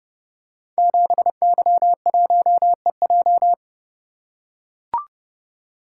Morse code, heard as a keyed tone, about 30 wpm, 700 Hz. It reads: 7Y1EJ
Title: Code Groups - 5 Letter:Numbers - 30wpm